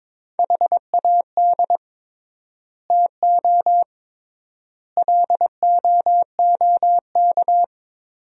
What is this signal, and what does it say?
Morse code, heard as a keyed tone, about 23 wpm, 700 Hz. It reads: HAD TO LOOK
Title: 3x Word-Spacing - Sentences from Top 100 Words - 22wpm